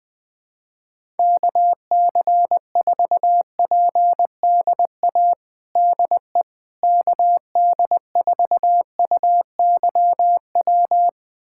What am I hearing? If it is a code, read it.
Morse code, heard as a keyed tone, about 21 wpm, 700 Hz. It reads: KC4PDA DE KD4UYW
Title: USA Callsign DE Callsign - 6 Characters - 20wpm